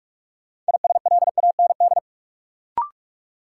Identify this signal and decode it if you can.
Morse code, heard as a keyed tone, about 41 wpm, 700 Hz. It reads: ISLAND
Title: Sentences from Top 500 Words - New Word Drill - 45wpm